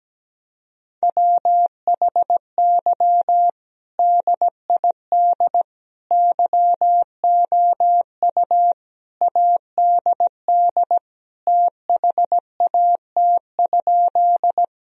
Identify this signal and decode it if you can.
Morse code, heard as a keyed tone, about 17 wpm, 700 Hz. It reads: WHY DID YOU ADD THAT?
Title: Sentences from Top 1000 Words - Review Part 1 - 17wpm